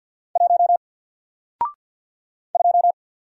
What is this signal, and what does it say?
Morse code, heard as a keyed tone, about 47 wpm, 700 Hz. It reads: 1 2
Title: Single Letter-Number Alt 50wpm